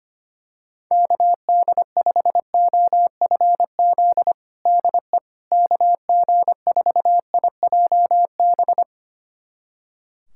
Morse code, heard as a keyed tone, about 25 wpm, 700 Hz. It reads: KD5OFZ DE KG4IJB
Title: USA Callsign DE Callsign - 6 Characters - 25wpm